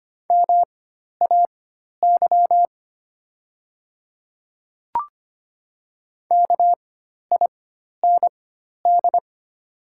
Morse code, heard as a keyed tone, about 25 wpm, spaced as for 12 wpm, 700 Hz. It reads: MAY KIND
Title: 4x Character Spacing - Top 200 Words - 25wpm